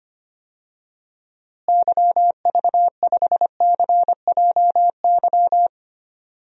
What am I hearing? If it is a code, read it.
Morse code, heard as a keyed tone, about 25 wpm, 700 Hz. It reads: YV5CJY
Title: DXCC-100 Call Signs - 6 Characters - Repeated 2x - 25wpm